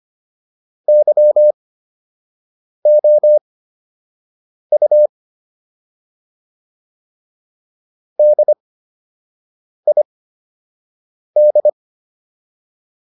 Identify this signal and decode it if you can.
Morse code, heard as a keyed tone, about 25 wpm, spaced as for 6 wpm, 600 Hz. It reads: YOU DID